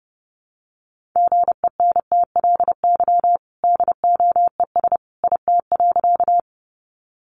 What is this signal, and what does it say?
Morse code, heard as a keyed tone, about 30 wpm, 700 Hz. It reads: GENTLY DOES IT.
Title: Sentences from Top 1500 Words 30wpm